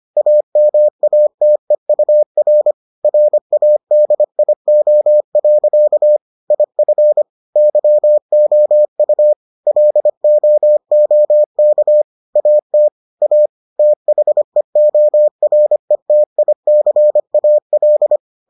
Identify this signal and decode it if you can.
Morse code, heard as a keyed tone, about 25 wpm, 600 Hz. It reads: AMATEUR RADIO. IF YOU LOOK AT A THEORETICAL